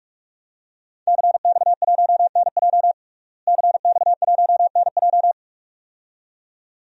Morse code, heard as a keyed tone, about 43 wpm, 700 Hz. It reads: CX1NJ CX1NJ
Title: DXCC-100 Call Signs - 5 Characters - Repeated 2x - 45wpm